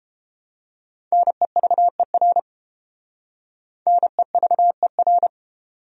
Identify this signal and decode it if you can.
Morse code, heard as a keyed tone, about 31 wpm, 700 Hz. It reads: NEVER NEVER